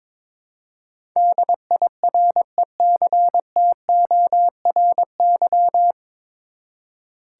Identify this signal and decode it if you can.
Morse code, heard as a keyed tone, about 22 wpm, 700 Hz. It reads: DIRECTORY
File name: 9 Letter Words - First 3 Letters Given - 22wpm